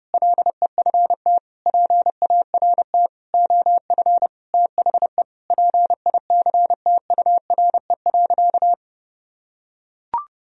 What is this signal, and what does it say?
Morse code, heard as a keyed tone, about 32 wpm, 700 Hz. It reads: LEFT PART OF THE PICTURE.
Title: Sentences from Top 1000 Words - Review Part 4 - 30wpm